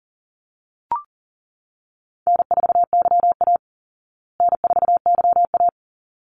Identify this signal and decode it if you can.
Morse code, heard as a keyed tone, about 40 wpm, 700 Hz. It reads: N4YA N4YA
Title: Call Signs Repeated 3x - 40wpm